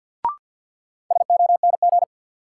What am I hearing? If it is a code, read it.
Morse code, heard as a keyed tone, about 47 wpm, 700 Hz. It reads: SONG
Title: Sentences from Top 400 Words - New Word Drill - 50wpm